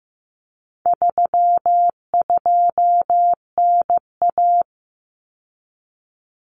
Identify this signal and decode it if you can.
Morse code, heard as a keyed tone, about 15 wpm, 700 Hz. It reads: 32NA
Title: Common Cut Numbers - 4-Digit 15wpm